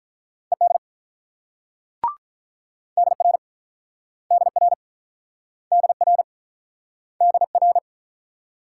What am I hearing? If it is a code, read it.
Morse code, heard as a keyed tone, about 47 wpm, 700 Hz. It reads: ED DR DR DR DR